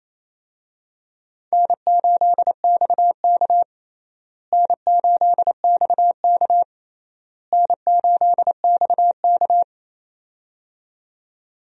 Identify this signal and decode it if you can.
Morse code, heard as a keyed tone, about 28 wpm, 700 Hz. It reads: N8XK N8XK N8XK
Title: USA Call Signs - 4 Characters - Repeated 3x - 28wpm